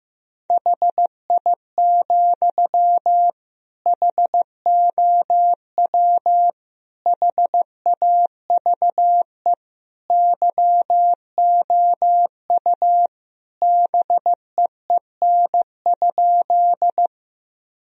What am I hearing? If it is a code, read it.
Morse code, heard as a keyed tone, about 15 wpm, 700 Hz. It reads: HI, HOW HAVE YOU BEEN?
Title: Sentences from Top 1000 Words - Review Part 3 - 15wpm